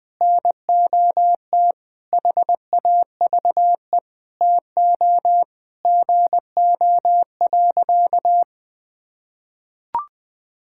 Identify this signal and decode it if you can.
Morse code, heard as a keyed tone, about 20 wpm, 700 Hz. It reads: NOT HAVE TO GO.
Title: Sentences with 4-Word Phrases - Spoken Beginning - 20wpm